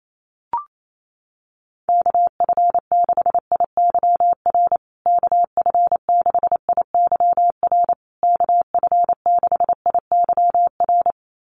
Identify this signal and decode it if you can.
Morse code, heard as a keyed tone, about 28 wpm, 700 Hz. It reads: KF6IYR KF6IYR KF6IYR
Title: USA Call Signs Repeated 3x 28wpm